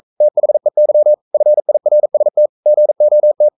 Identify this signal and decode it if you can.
Morse code, heard as a keyed tone, about 42 wpm, 600 Hz. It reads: THEY FIRST GOT
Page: Romanen Never af Ken Follet på engelsk.